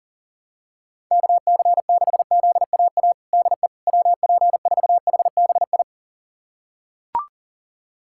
Morse code, heard as a keyed tone, about 37 wpm, 700 Hz. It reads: KC6ZAA DE WP4HBI
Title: USA Callsign DE Callsign - 6 Characters - 40wpm